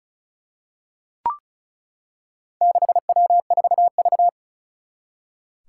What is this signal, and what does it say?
Morse code, heard as a keyed tone, about 36 wpm, 700 Hz. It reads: 6W4V
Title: Four Challenging Letter-Number Combinations 35wpm